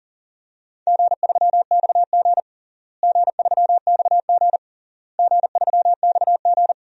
Morse code, heard as a keyed tone, about 40 wpm, 700 Hz. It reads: G3XG G3XG G3XG
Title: DXCC-100 Call Signs - 4 Characters - Repeated 3x - 40wpm